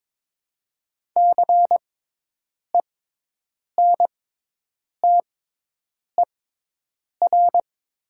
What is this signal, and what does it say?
Morse code, heard as a keyed tone, about 22 wpm, spaced as for 8 wpm, 700 Hz. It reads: CENTER
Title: 6x Character Spacing - Top 500 Words - 22wpm